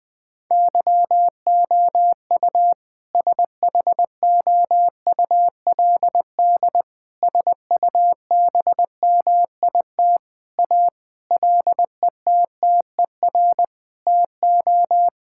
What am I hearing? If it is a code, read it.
Morse code, heard as a keyed tone, about 21 wpm, 700 Hz. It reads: YOU SHOULD SUBMIT A LETTER TO
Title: Sentences with 6-Word Phrases - 20wpm